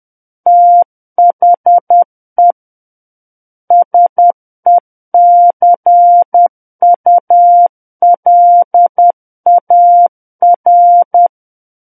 Morse code, heard as a keyed tone, about 10 wpm, 700 Hz. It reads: THE SECULAR
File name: Book One / 01-Chapter 1 - 10 WPM